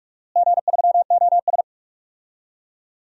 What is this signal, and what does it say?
Morse code, heard as a keyed tone, about 46 wpm, 700 Hz. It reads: G3OS
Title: DXCC-100 Call Signs - 4 Characters - Repeated 2x - 45wpm